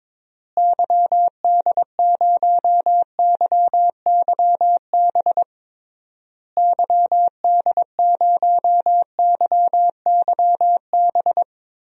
Morse code, heard as a keyed tone, about 22 wpm, 700 Hz. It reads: YD0YYB YD0YYB
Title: DXCC-100 Call Signs - 6 Characters - Repeated 3x - 22wpm